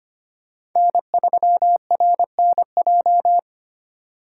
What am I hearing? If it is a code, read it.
Morse code, heard as a keyed tone, about 25 wpm, 700 Hz. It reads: N3RNJ
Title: USA Call Signs - 5 Characters - 2x Character Spacing - 25wpm